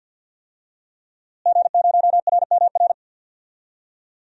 Morse code, heard as a keyed tone, about 51 wpm, 700 Hz. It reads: G0RMR
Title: DXCC-100 Call Signs - 5 Characters - Repeated 3x - 50wpm